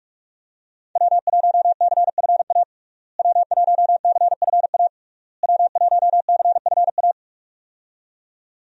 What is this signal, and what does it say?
Morse code, heard as a keyed tone, about 42 wpm, 700 Hz. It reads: W1CFA W1CFA W1CFA
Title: USA Call Signs - 5 Characters - Repeated 3x - 45wpm